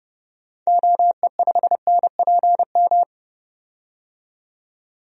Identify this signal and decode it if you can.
Morse code, heard as a keyed tone, about 30 wpm, 700 Hz. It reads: OE5NPM
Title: DXCC-100 Call Signs - 6 Characters - 30wpm